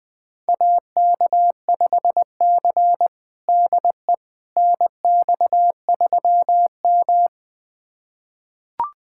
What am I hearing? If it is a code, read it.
Morse code, heard as a keyed tone, about 20 wpm, 700 Hz. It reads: AK5C DE NX3M
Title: USA Callsign DE Callsign - 4 Characters - 20wpm